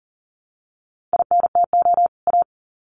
Morse code, heard as a keyed tone, about 40 wpm, 700 Hz. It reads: INTO A
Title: Sets of 2 Words 40wpm